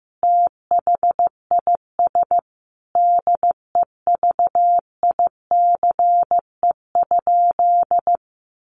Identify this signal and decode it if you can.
Morse code, heard as a keyed tone, about 15 wpm, 700 Hz. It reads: THIS DEVICE?
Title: Sentences from Top 1500 Words 15wpm